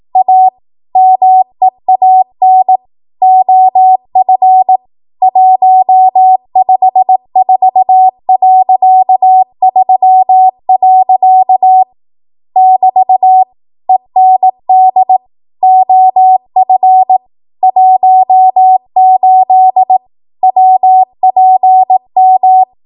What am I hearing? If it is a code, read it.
Morse code, heard as a keyed tone, about 18 wpm, 750 Hz. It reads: A MEAN OF 154.3. <BT> END OF 18 WPM
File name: Morse / 18WPM